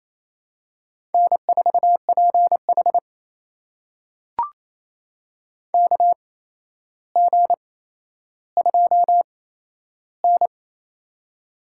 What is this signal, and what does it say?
Morse code, heard as a keyed tone, about 28 wpm, spaced as for 12 wpm, 700 Hz. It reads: N4PH KG2N
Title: USA Call Signs - 4 Characters - 8x Character Spacing - 28wpm